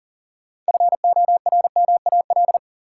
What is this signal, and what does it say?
Morse code, heard as a keyed tone, about 40 wpm, 700 Hz. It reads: FORMAL
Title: Sentences from Top 1300 Words - New Word Drill Alt - 40wpm